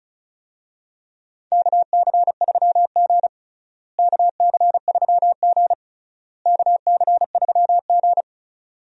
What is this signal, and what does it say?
Morse code, heard as a keyed tone, about 34 wpm, 700 Hz. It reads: KC3G KC3G KC3G
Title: USA Call Signs - 4 Characters - Repeated 3x - 35wpm